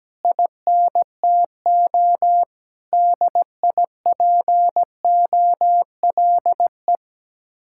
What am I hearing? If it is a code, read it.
Morse code, heard as a keyed tone, about 17 wpm, 700 Hz. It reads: INTO DIPOLE
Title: Speed-Racing Common QSO Phrases Encore - 17wpm